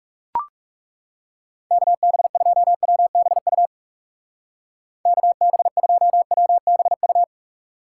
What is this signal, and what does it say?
Morse code, heard as a keyed tone, about 36 wpm, 700 Hz. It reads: KB2WBU KB2WBU
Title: Speed Racing - USA Call Signs 30wpm